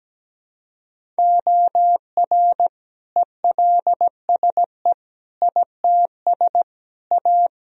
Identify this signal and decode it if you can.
Morse code, heard as a keyed tone, about 17 wpm, 700 Hz. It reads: OR ELSE ITS A
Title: Sentences from Top 1000 Words - Review Part 1 - 17wpm